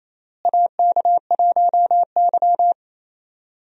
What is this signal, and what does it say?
Morse code, heard as a keyed tone, about 28 wpm, 700 Hz. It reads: AK1Y
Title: USA Call Signs - 4 Characters - Repeated 3x - 28wpm